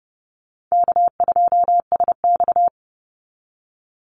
Morse code, heard as a keyed tone, about 30 wpm, 700 Hz. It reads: K2SX
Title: Call Signs Repeated 2x - 30wpm